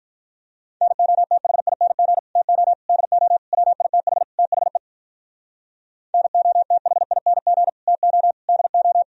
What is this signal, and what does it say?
Morse code, heard as a keyed tone, about 52 wpm, 700 Hz. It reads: NOTHING TO DO WITH THE NOTHING TO DO